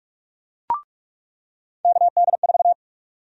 Morse code, heard as a keyed tone, about 42 wpm, 700 Hz. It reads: KD4
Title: USA Call Sign Prefixes - 45wpm